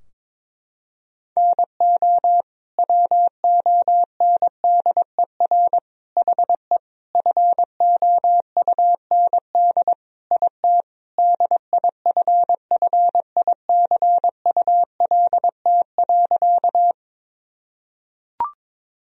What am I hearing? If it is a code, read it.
Morse code, heard as a keyed tone, about 22 wpm, 700 Hz. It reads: NO WONDER HE FOUND IT DIFFICULT.
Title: Sentences from Top 1000 Words - Review Part 3 - 22wpm